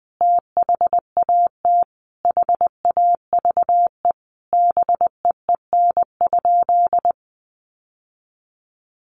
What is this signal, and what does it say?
Morse code, heard as a keyed tone, about 19 wpm, 700 Hz. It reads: THAT HAVE BEEN?
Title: Sentences from Top 800 Words 20wpm